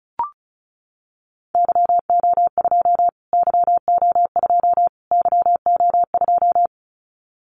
Morse code, heard as a keyed tone, about 34 wpm, 700 Hz. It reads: YO2 YO2 YO2
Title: Callsign Prefixes for Top 25 DXCC - Repeated 3x - 35wpm